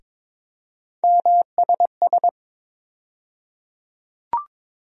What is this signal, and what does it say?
Morse code, heard as a keyed tone, about 22 wpm, 700 Hz. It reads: MSS
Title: USA Call Sign Postfixes - 3 Letters 22wpm